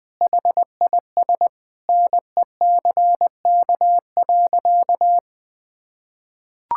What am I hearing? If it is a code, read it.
Morse code, heard as a keyed tone, about 21 wpm, 700 Hz. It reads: HIS NECK.
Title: Sentences with 4-Word Phrases - Spoken Beginning - 20wpm